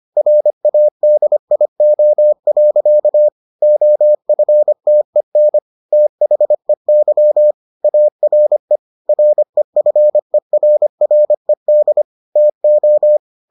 Morse code, heard as a keyed tone, about 25 wpm, 600 Hz. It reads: RADIO. OFTEN THEY ARE REFERRED TO